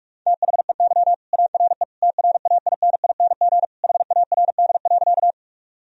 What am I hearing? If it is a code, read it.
Morse code, heard as a keyed tone, about 46 wpm, 700 Hz. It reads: THEY ARE TRAINING HARD.
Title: Sentences from Top 1000 Words - Review Part 4 - 45wpm